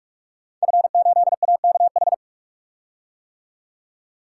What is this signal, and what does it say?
Morse code, heard as a keyed tone, about 44 wpm, 700 Hz. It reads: F8AKH
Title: Code Groups - 5 Letter:Numbers - 45wpm